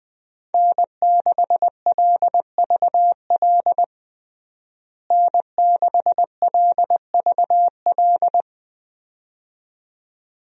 Morse code, heard as a keyed tone, about 20 wpm, 700 Hz. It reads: N6LVL N6LVL
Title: USA Call Signs - 5 Characters - Repeated 2x - 20wpm